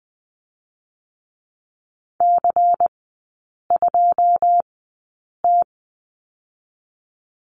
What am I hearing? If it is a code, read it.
Morse code, heard as a keyed tone, about 20 wpm, 700 Hz. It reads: C 2 T
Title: 100 Lesson 17e - Letter u - Three Characters 20wpm